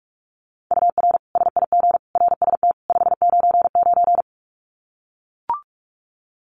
Morse code, heard as a keyed tone, about 44 wpm, 700 Hz. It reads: UR SIG RST 599
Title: Common QSO Phrases 45wpm